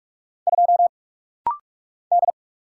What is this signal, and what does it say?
Morse code, heard as a keyed tone, about 44 wpm, 690 Hz. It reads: 2 D
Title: Single Letter-Number - Rapid-Fire 45wpm